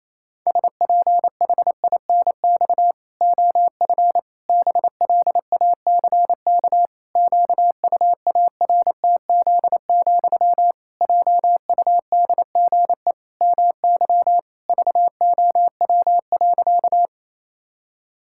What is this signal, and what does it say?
Morse code, heard as a keyed tone, about 28 wpm, 700 Hz. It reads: SPHINX OF BLACK QUARTZ, JUDGE MY VOW.
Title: Pangram Challenge v2 - 28wpm